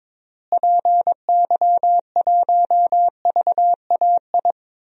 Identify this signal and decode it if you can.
Morse code, heard as a keyed tone, about 21 wpm, 700 Hz. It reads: PY1VAI
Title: DXCC-100 Call Signs - 6 Characters - Repeated 2x - 22wpm